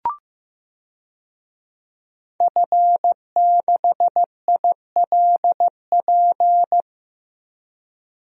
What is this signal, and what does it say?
Morse code, heard as a keyed tone, about 15 wpm, 700 Hz. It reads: F6ILP
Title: DXCC-100 Call Signs - 5 Characters - 15wpm